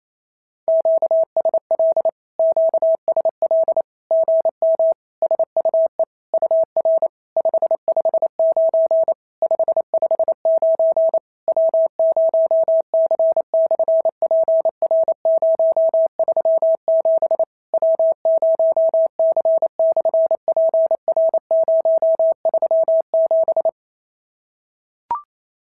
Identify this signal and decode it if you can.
Morse code, heard as a keyed tone, about 28 wpm, 650 Hz. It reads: QSL QSL GM SUE UR 559 559 W0C/PR037 W0C/PR037
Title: SOTA QSO Practice 28wpm